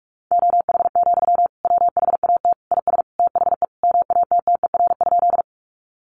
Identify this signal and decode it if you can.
Morse code, heard as a keyed tone, about 44 wpm, 700 Hz. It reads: OH, WHAT IS THE MATTER?
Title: Sentences from Top 700 Words 45wpm